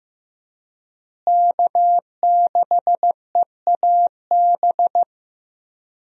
Morse code, heard as a keyed tone, about 15 wpm, 700 Hz. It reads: K6EAB
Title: USA Call Signs - 5 Characters - Repeated 2x - 15wpm